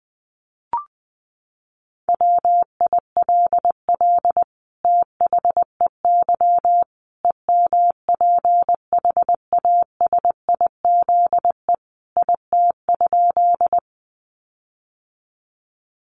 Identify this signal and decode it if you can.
Morse code, heard as a keyed tone, about 19 wpm, 700 Hz. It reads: WILL THEY EMPHASIZE IT?
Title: Sentences from Top 1500 Words 20wpm